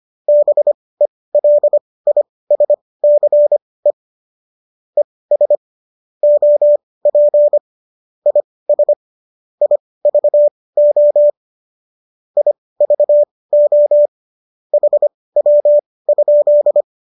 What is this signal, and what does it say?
Morse code, heard as a keyed tone, about 25 wpm, spaced as for 18 wpm, 590 Hz. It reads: BELISCE ES OP IS IVO IVO HW?